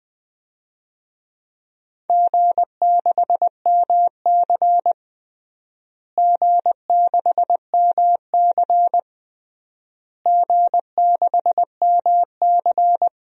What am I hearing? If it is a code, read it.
Morse code, heard as a keyed tone, about 20 wpm, 700 Hz. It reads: G6MC G6MC G6MC
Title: DXCC-100 Call Signs - 4 Characters - Repeated 3x - 20wpm